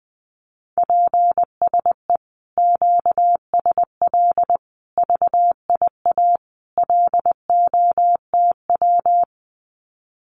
Morse code, heard as a keyed tone, about 19 wpm, 700 Hz. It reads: PSE QSL VIA LOTW
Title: Common QSO Phrases Encore 20wpm